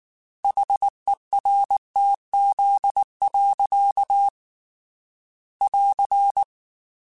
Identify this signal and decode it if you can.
Morse code, heard as a keyed tone, about 19 wpm, 800 Hz. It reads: HERTZ. <AR>
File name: Technical Subjects / Technical Subjects-mp3s / Reactance @18